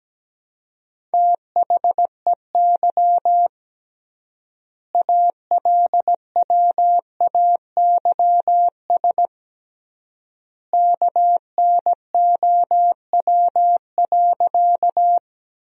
Morse code, heard as a keyed tone, about 17 wpm, 700 Hz. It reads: THEY ALWAYS KNOW.
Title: 3x Word-Spacing - Sentences from Top 300 Words - 17wpm